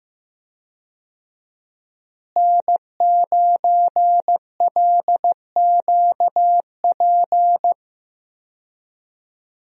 Morse code, heard as a keyed tone, about 15 wpm, 700 Hz. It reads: N9LQP
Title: USA Call Signs - 5 Characters - Repeated 3x - 15wpm